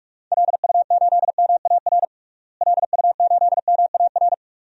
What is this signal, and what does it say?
Morse code, heard as a keyed tone, about 43 wpm, 700 Hz. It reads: LU8MAR LU8MAR
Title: DXCC-100 Call Signs - 6 Characters - Repeated 2x - 45wpm